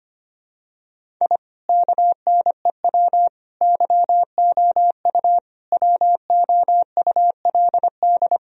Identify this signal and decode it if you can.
Morse code, heard as a keyed tone, about 25 wpm, 700 Hz. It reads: I KNEW YOU WOULD
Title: Sentences from Top 1000 Words - Review Part 1 - 25wpm